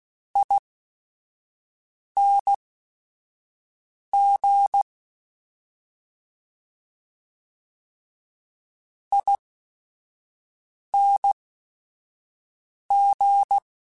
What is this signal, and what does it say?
Morse code, heard as a keyed tone, about 16 wpm, spaced as for 5 wpm, 800 Hz. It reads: ING ING